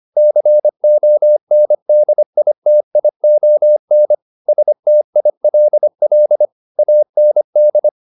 Morse code, heard as a keyed tone, about 25 wpm, 600 Hz. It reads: CONDITION STILL AND
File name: the old man and the sea part 1 25wpm